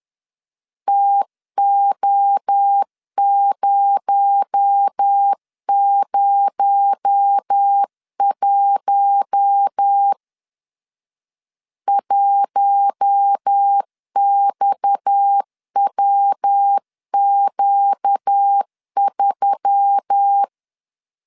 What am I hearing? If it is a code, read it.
Morse code, heard as a keyed tone, about 10 wpm, 800 Hz. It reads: TO001 1XWQ3
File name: LEZIONI / 6 PKX0 / 6 PKX0 10wpm 2707